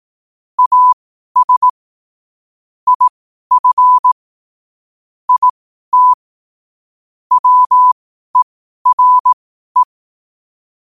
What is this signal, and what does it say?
Morse code, heard as a keyed tone, about 18 wpm, spaced as for 12 wpm, 1000 Hz. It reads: AS IF IT WERE
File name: qotdfiles / 12wpm